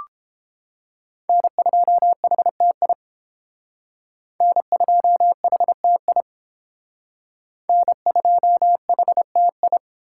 Morse code, heard as a keyed tone, about 28 wpm, 700 Hz. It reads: N2HTI N2HTI N2HTI